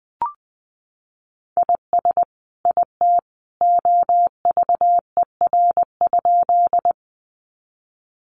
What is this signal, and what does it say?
Morse code, heard as a keyed tone, about 20 wpm, 700 Hz. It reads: IS IT OVER?
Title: Sentences from Top 100 Words 20wpm